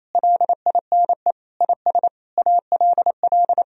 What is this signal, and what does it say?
Morse code, heard as a keyed tone, about 28 wpm, 700 Hz. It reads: LINE IS ALL
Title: Sentences from Top 1000 Words - Review Part 1 - 28wpm